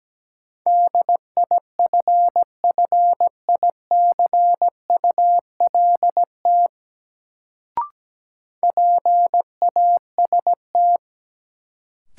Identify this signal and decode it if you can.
Morse code, heard as a keyed tone, about 17 wpm, 700 Hz. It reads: DIFFICULT PAST
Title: Sentences from Top 600 Words - New Word Drill - 17wpm